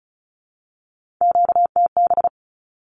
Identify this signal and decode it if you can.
Morse code, heard as a keyed tone, about 34 wpm, 700 Hz. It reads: QTB
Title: Q-Codes Repeated 3x 35wpm